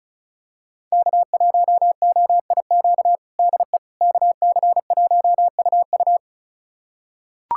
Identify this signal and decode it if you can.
Morse code, heard as a keyed tone, about 34 wpm, 700 Hz. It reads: K1OIQ DE KC1UU
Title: USA Callsign DE Callsign - 5 Characters - 35wpm